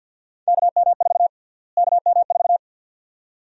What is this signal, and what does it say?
Morse code, heard as a keyed tone, about 51 wpm, 700 Hz. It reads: KM4 KM4
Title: USA Call Sign Prefixes - Repeated 2x - 50wpm